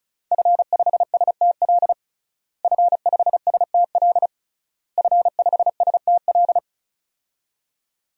Morse code, heard as a keyed tone, about 36 wpm, 700 Hz. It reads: F5STL F5STL F5STL
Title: DXCC-100 Call Signs - 5 Characters - Repeated 3x - 35wpm